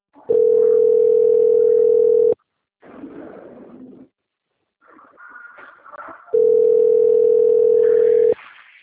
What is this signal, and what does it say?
Ringback tone (North America), 440 + 480 Hz.